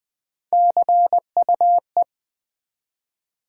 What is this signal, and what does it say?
Morse code, heard as a keyed tone, about 20 wpm, 700 Hz. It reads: CUE
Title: Build A Word - 1 Letter At A Time - 3-Letter Words - 20wpm